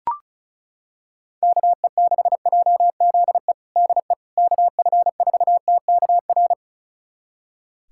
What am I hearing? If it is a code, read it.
Morse code, heard as a keyed tone, about 36 wpm, 700 Hz. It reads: KE6JZE DE KF4TKR
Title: USA Callsign DE Callsign - 6 Characters - 35wpm